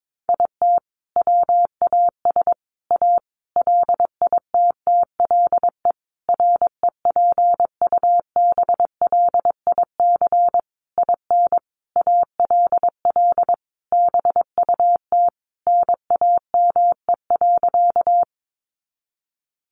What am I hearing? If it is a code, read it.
Morse code, heard as a keyed tone, about 22 wpm, 700 Hz. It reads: IT WAS A LITTLE REPUBLIC IN ALL BUT NAME.
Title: Sentences from Top 1500 Words 22wpm